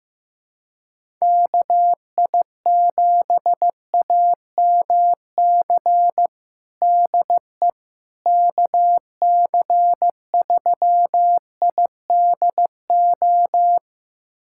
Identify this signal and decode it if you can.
Morse code, heard as a keyed tone, about 15 wpm, 700 Hz. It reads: KI7AMC DE KC3IDO
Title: USA Callsign DE Callsign 15wpm